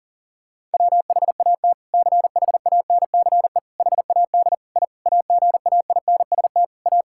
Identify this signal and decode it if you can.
Morse code, heard as a keyed tone, about 38 wpm, 700 Hz. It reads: WHAT CHANCE HAD I AGAINST A
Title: Sentences from Top 1000 Words - Review Part 6 - 40wpm